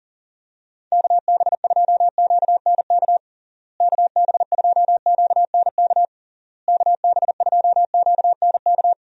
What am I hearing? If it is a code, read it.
Morse code, heard as a keyed tone, about 40 wpm, 700 Hz. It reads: KB2QNK KB2QNK KB2QNK
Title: USA Call Signs - 6 Characters - Repeated 3x - 40wpm